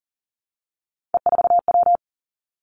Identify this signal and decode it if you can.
Morse code, heard as a keyed tone, about 38 wpm, 700 Hz. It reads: E4W
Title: Three Challenging Letter-Number Combinations 40wpm